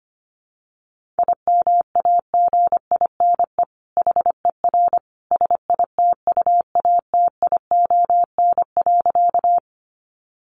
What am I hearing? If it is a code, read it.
Morse code, heard as a keyed tone, about 25 wpm, 700 Hz. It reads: IMAGINE HER SITUATION.
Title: Sentences from Top 1400 Words 25wpm